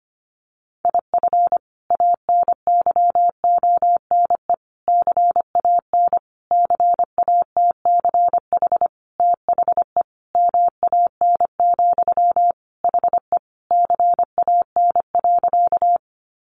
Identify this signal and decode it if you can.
Morse code, heard as a keyed tone, about 25 wpm, 700 Hz. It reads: IF ANYONE CAN CATCH THE MAN, HE CAN.
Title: Sentences from Top 1500 Words 25wpm